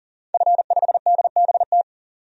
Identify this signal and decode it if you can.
Morse code, heard as a keyed tone, about 40 wpm, 700 Hz. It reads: F5DBT
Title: DXCC-100 Call Signs - 5 Characters - 40wpm